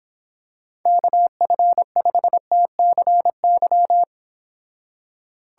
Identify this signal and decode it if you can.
Morse code, heard as a keyed tone, about 26 wpm, 700 Hz. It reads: KF5TCY